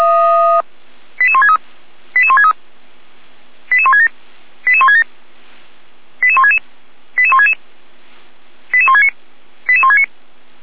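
Five-tone selective call (ZVEI), heard as quick runs of five tones, tones of about 70 ms. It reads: 88163 88163 88167 88167 88166 88166 88169 88169
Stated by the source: ZVEI five-tone call 88457 received by SDR.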